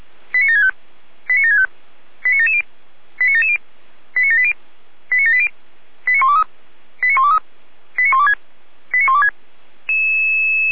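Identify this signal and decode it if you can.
Five-tone selective call (ZVEI), heard as quick runs of five tones, tones of about 70 ms. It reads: 89765 89765 89770 89770 89780 89780 89123 89123 89126 89126
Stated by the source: ZVEI five-tone selective call 89756 received on an SDR.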